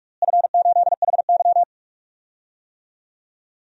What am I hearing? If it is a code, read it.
Morse code, heard as a keyed tone, about 46 wpm, 700 Hz. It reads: F8HY
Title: DXCC-100 Call Signs - 4 Characters - Repeated 3x - 45wpm